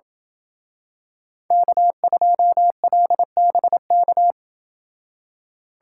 Morse code, heard as a keyed tone, about 27 wpm, 700 Hz. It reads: K2LBK